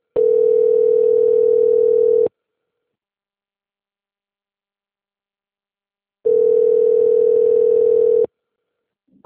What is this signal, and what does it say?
Ringback tone (North America), 440 + 480 Hz.
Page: Rio 3